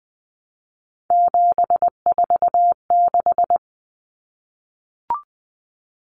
Morse code, heard as a keyed tone, about 20 wpm, 700 Hz. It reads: 746
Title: Common Cut Numbers - 3-Digit 20wpm